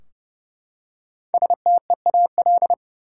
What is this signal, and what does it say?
Morse code, heard as a keyed tone, about 30 wpm, 700 Hz. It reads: STEAL
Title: Five Letter Words - 30wpm